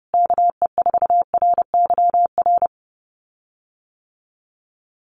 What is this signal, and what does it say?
Morse code, heard as a keyed tone, about 30 wpm, 700 Hz. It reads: KE4RYR
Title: USA Call Signs Repeated 3x 30wpm